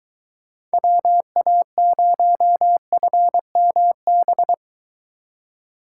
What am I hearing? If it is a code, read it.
Morse code, heard as a keyed tone, about 22 wpm, 700 Hz. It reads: WA0FMB